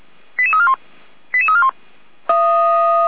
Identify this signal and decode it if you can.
Five-tone selective call (ZVEI), heard as quick runs of five tones, tones of about 70 ms. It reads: 88341 88341